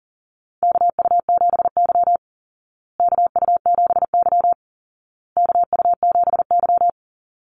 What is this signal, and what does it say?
Morse code, heard as a keyed tone, about 40 wpm, 700 Hz. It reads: KU7Y KU7Y KU7Y
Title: Call Signs Repeated 2x - 40wpm